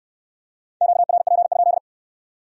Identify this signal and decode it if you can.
Morse code, heard as a keyed tone, about 62 wpm, 700 Hz. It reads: QRP?